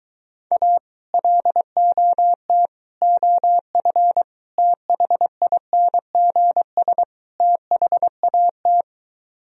Morse code, heard as a keyed tone, about 23 wpm, 700 Hz. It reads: A LOT OF THINGS THAT